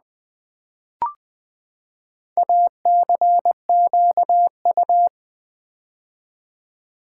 Morse code, heard as a keyed tone, about 20 wpm, 700 Hz. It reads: ACQU
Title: Build A Word - 4 Letters At A Time - 11-Letter Words - 20wpm